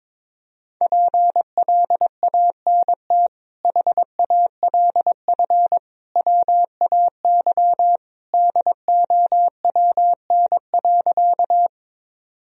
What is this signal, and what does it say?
Morse code, heard as a keyed tone, about 22 wpm, 700 Hz. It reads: PLANT HALF WAY DOWN.
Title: Sentences from Top 1000 Words - Review Part 1 - 22wpm